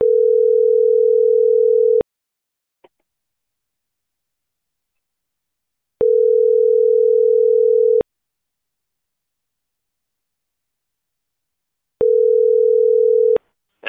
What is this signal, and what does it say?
Ringback tone (North America), 440 + 480 Hz.